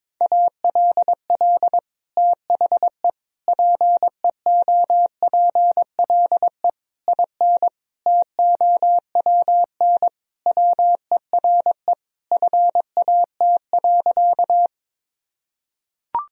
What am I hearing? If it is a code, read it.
Morse code, heard as a keyed tone, about 22 wpm, 700 Hz. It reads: ALL THE PEOPLE IN TOWN WERE FAT.
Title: Sentences from Top 1000 Words - Review Part 4 - 22wpm